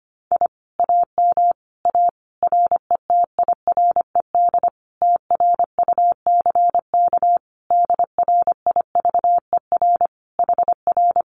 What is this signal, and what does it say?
Morse code, heard as a keyed tone, about 25 wpm, 700 Hz. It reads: I AM A RETIRED TRUCK DRIVER HR
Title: Common QSO Phrases Encore 25wpm